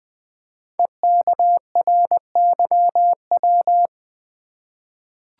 Morse code, heard as a keyed tone, about 21 wpm, 700 Hz. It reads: EKRYW
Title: Code Groups - 5 Letters - 20wpm